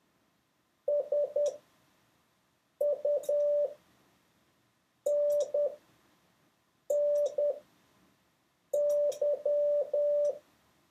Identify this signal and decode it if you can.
Morse code, heard as a keyed tone, about 10 wpm, spaced as for 5 wpm, 580 Hz. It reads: SUNNY
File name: 6of6-test